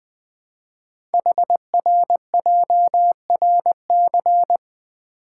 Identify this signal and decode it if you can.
Morse code, heard as a keyed tone, about 20 wpm, 700 Hz. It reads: HRJRC
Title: Code Groups - 5 Letters - 20wpm